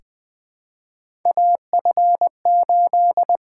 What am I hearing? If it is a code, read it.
Morse code, heard as a keyed tone, about 20 wpm, 700 Hz. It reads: AF8
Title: USA Call Signs - 6x Character Spacing - 20wpm